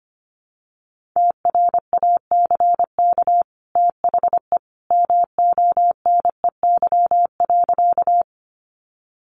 Morse code, heard as a keyed tone, about 25 wpm, 700 Hz. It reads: TRACK THE MONEY.
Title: Sentences from Top 1100 Words 25wpm